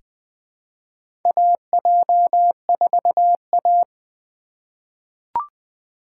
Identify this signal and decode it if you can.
Morse code, heard as a keyed tone, about 20 wpm, 700 Hz. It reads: AJ4A
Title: USA Call Signs - 4 Characters - 2x Character Spacing - 20wpm